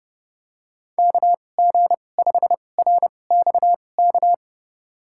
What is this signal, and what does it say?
Morse code, heard as a keyed tone, about 30 wpm, spaced as for 22 wpm, 700 Hz. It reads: KG5RXK
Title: USA Call Signs - 6 Characters - 2x Character Spacing - 30wpm